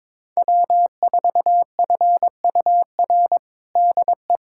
Morse code, heard as a keyed tone, about 22 wpm, 700 Hz. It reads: W4FUR DE
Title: USA Callsign DE Callsign - 5 Characters - 22wpm